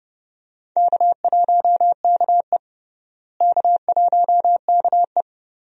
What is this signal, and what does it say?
Morse code, heard as a keyed tone, about 30 wpm, 700 Hz. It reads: K1KE K1KE
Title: USA Call Signs - 4 Characters - Repeated 3x - 30wpm